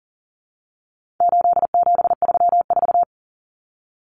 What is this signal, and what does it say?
Morse code, heard as a keyed tone, about 40 wpm, 700 Hz. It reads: 8734
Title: Common Cut Numbers - 4-Digit 40wpm